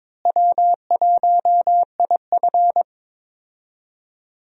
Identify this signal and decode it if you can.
Morse code, heard as a keyed tone, about 22 wpm, 700 Hz. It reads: W1IF
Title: USA Call Signs - 4 Characters - Repeated 3x - 22wpm